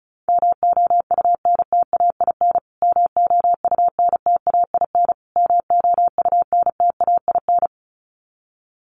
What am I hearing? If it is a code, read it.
Morse code, heard as a keyed tone, about 34 wpm, 700 Hz. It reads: MOUNTAIN MOUNTAIN MOUNTAIN
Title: Top 300 Words Repeated 35wpm